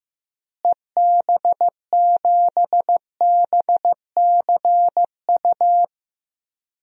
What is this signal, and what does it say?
Morse code, heard as a keyed tone, about 15 wpm, 700 Hz. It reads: EB7BCU
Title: DXCC-100 Call Signs - 6 Characters - Repeated 3x - 15wpm